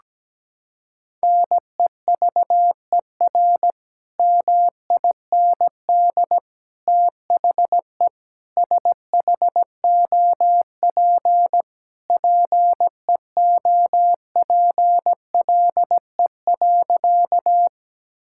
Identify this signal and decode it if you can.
Morse code, heard as a keyed tone, about 17 wpm, 700 Hz. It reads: NEVER MIND THE SHOP PEOPLE.
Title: Sentences from Top 1600 Words 17wpm